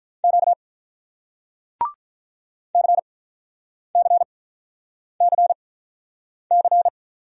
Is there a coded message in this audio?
Yes: Morse code from 0.2 to 6.9 s, heard as a keyed tone, about 48 wpm, spaced as for 9 wpm, 700 Hz. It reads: Q CCCC